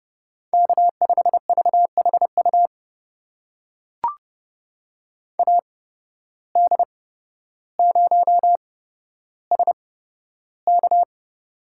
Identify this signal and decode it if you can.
Morse code, heard as a keyed tone, about 30 wpm, spaced as for 13 wpm, 700 Hz. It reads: K5VHU AD0SK
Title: USA Call Signs - 5 Characters - 8x Character Spacing - 30wpm